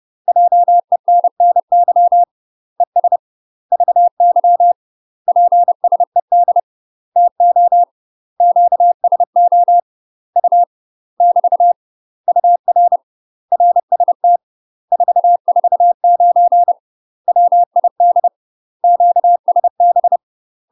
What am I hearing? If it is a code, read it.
Morse code, heard as a keyed tone, about 30 wpm, 700 Hz. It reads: JENNY ES VY PSED TO QSO U <BT> UR RST 449 WID QSB
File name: UK MASTER QSOs TRAINING / 30WPM / UK MASTER QSOs 0033